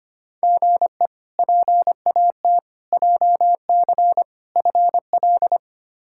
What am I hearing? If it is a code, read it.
Morse code, heard as a keyed tone, about 25 wpm, 700 Hz. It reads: GE PAT JC FL
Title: K1USN SST QSO Practice - 25wpm